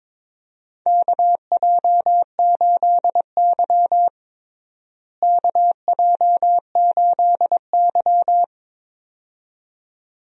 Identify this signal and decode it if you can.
Morse code, heard as a keyed tone, about 22 wpm, 700 Hz. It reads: KJ8Y KJ8Y
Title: USA Call Signs - 4 Characters - Repeated 2x - 22wpm